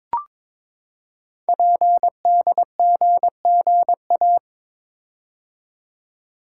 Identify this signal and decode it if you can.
Morse code, heard as a keyed tone, about 22 wpm, 700 Hz. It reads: PDGGA
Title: Code Groups - 5 Letters - 22wpm